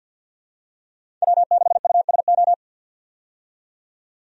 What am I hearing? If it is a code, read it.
Morse code, heard as a keyed tone, about 51 wpm, 700 Hz. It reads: W6USO
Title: USA Call Signs - 5 Characters - 50wpm